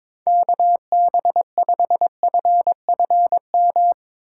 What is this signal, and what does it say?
Morse code, heard as a keyed tone, about 22 wpm, 700 Hz. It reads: KB5FFM
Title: USA Call Signs - 4x Character Spacing - 22wpm